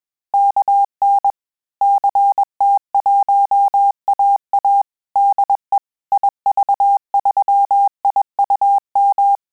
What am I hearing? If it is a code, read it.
Morse code, heard as a keyed tone, about 21 wpm, 800 Hz. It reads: KN CT1AA DE IV3IUM
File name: QSO CW 100 CPM